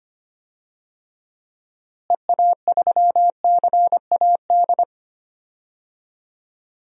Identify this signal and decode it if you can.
Morse code, heard as a keyed tone, about 25 wpm, 700 Hz. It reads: EA3CAD
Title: DXCC-100 Call Signs - 6 Characters - Repeated 3x - 25wpm